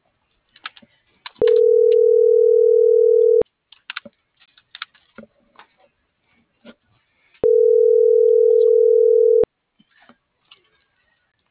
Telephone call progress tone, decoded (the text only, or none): ringback tone